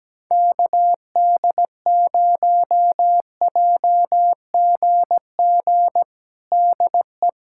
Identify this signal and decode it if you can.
Morse code, heard as a keyed tone, about 17 wpm, 700 Hz. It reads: KD0JGG DE
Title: USA Callsign DE Callsign 17wpm